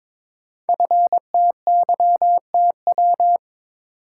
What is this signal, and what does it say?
Morse code, heard as a keyed tone, about 22 wpm, 700 Hz. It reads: FTYTW
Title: Code Groups - 5 Letters - 22wpm